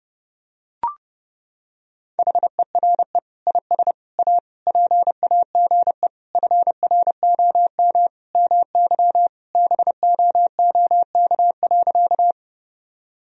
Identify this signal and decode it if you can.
Morse code, heard as a keyed tone, about 30 wpm, 700 Hz. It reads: HERE IS A PAGE FROM MY BOOK.
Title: Sentences from Top 1000 Words - Review Part 1 - 30wpm